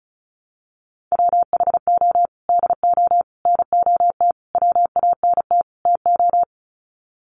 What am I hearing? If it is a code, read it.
Morse code, heard as a keyed tone, about 34 wpm, 700 Hz. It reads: WHO DO NOT WANT TO
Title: Sets of 5 Words Encore2 35wpm